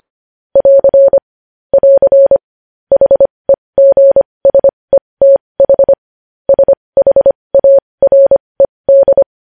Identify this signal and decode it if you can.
Morse code, heard as a keyed tone, about 25 wpm, spaced as for 20 wpm, 550 Hz. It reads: <AR> <AR> HEGSETH SHARED
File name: podcast / News headlines-1x-25x20